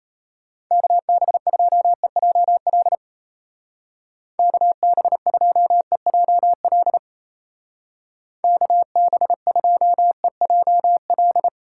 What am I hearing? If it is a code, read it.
Morse code, heard as a keyed tone, about 34 wpm, 700 Hz. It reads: KB2EJL KB2EJL KB2EJL
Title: Speed Racing - USA 2x3 Call Signs 28wpm